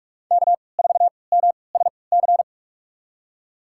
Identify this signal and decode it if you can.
Morse code, heard as a keyed tone, about 46 wpm, spaced as for 25 wpm, 700 Hz. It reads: K4MSC
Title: USA Call Signs - 3x Character Spacing - 45wpm